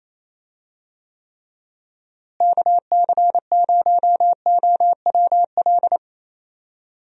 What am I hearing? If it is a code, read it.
Morse code, heard as a keyed tone, about 28 wpm, 700 Hz. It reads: KC0OWL
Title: USA Call Signs - 6 Characters - Repeated 3x - 28wpm